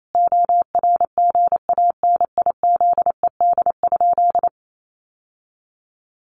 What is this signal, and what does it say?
Morse code, heard as a keyed tone, about 28 wpm, 700 Hz. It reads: ORGANIZED?
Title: Sentences from Top 1300 Words 28wpm